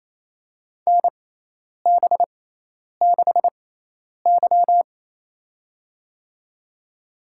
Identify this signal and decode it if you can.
Morse code, heard as a keyed tone, about 27 wpm, spaced as for 10 wpm, 700 Hz. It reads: NB6Y
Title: USA Call Signs - 4 Characters - 6x Character Spacing - 28wpm